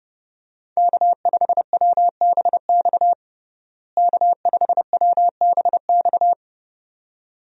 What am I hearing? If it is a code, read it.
Morse code, heard as a keyed tone, about 32 wpm, 700 Hz. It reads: K5WBX K5WBX
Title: USA Call Signs - 5 Characters - Repeated 2x - 30wpm